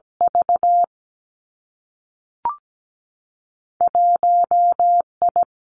Morse code, heard as a keyed tone, about 17 wpm, 700 Hz. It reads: V 1 I
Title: Two Challenging Letter-Number Combinations 17wpm